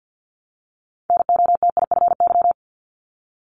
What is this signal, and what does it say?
Morse code, heard as a keyed tone, about 50 wpm, 700 Hz. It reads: NOTIFY
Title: Multi-letter Phonemes 50wpm